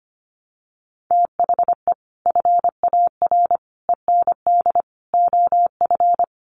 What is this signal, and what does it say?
Morse code, heard as a keyed tone, about 25 wpm, 700 Hz. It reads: THE FAR END OF
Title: Sets of 5 Words 25wpm